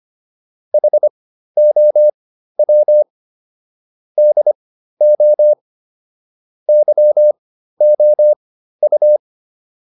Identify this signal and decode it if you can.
Morse code, heard as a keyed tone, about 25 wpm, spaced as for 13 wpm, 600 Hz. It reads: HOW DO YOU